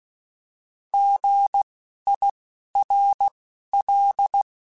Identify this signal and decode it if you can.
Morse code, heard as a keyed tone, about 16 wpm, 800 Hz. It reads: G I R L
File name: Misc / Misc-mp3s / Misc @10